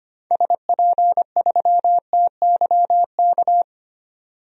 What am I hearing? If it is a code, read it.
Morse code, heard as a keyed tone, about 25 wpm, 700 Hz. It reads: SP3TYK
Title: DXCC-100 Call Signs - 6 Characters - 25wpm